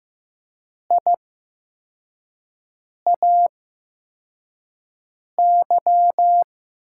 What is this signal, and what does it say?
Morse code, heard as a keyed tone, about 15 wpm, spaced as for 4 wpm, 700 Hz. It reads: IAY
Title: USA Call Signs - 6 Characters - 8x Character Spacing - 15wpm